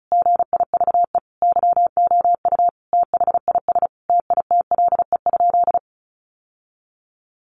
Morse code, heard as a keyed tone, about 34 wpm, 700 Hz. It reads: GIVE YOU THIS TITLE?
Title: Sentences from Top 1200 Words 35wpm